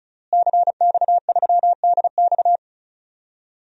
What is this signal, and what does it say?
Morse code, heard as a keyed tone, about 36 wpm, 700 Hz. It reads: CX3DX
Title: DXCC-100 Call Signs - 5 Characters - Repeated 3x - 35wpm